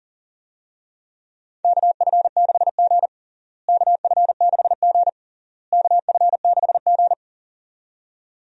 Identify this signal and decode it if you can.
Morse code, heard as a keyed tone, about 40 wpm, 700 Hz. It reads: KF6G KF6G KF6G
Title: USA Call Signs - 4 Characters - Repeated 3x - 40wpm